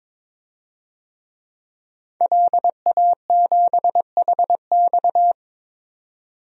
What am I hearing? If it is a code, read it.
Morse code, heard as a keyed tone, about 22 wpm, 700 Hz. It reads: LA7HX
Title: DXCC-100 Call Signs - 5 Characters - 22wpm